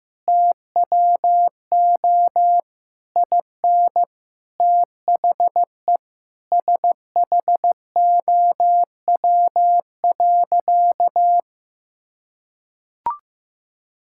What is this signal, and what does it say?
Morse code, heard as a keyed tone, about 15 wpm, 700 Hz. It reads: TWO IN THE SHOW.
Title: Sentences from Top 1000 Words - Review Part 6 - 15wpm